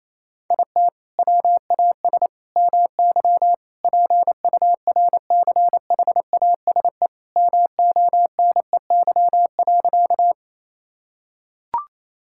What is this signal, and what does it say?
Morse code, heard as a keyed tone, about 28 wpm, 700 Hz. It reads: IT WAS MY PURCHASE MONEY.
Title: Sentences from Top 1000 Words - Review Part 5 - 28wpm